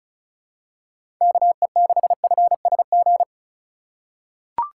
Morse code, heard as a keyed tone, about 36 wpm, 700 Hz. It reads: KE6FSG
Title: USA Call Signs - 4x Character Spacing - 35wpm